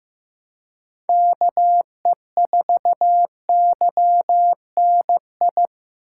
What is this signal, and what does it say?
Morse code, heard as a keyed tone, about 18 wpm, 700 Hz. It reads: KE4YNI
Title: Speed Racing - USA 2x3 Call Signs 15wpm